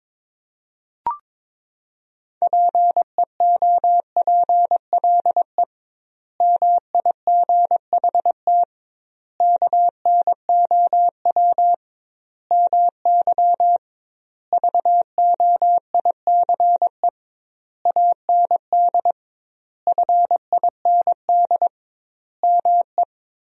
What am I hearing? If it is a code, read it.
Morse code, heard as a keyed tone, about 22 wpm, 700 Hz. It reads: PEOPLE MIGHT KNOW MY VOICE AND FIND ME
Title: 2x Word-Spacing - Sentences from Top 500 Words - 22wpm